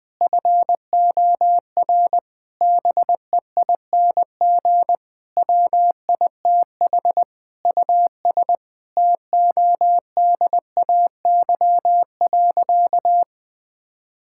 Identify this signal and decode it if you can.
Morse code, heard as a keyed tone, about 20 wpm, 700 Hz. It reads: FOR BEING WITH US TODAY.
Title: Sentences with 8-Word Phrases - Spoken Beginning - 20wpm